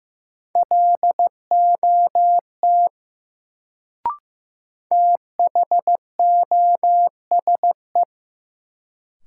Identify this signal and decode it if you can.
Morse code, heard as a keyed tone, about 15 wpm, 700 Hz. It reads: LOT THOSE
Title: Sets of 2 Words - New Word Drill - 15wpm